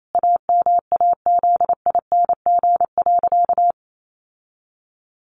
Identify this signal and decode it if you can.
Morse code, heard as a keyed tone, about 28 wpm, 700 Hz. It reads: AMAZING.
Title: Sentences from Top 1500 Words 28wpm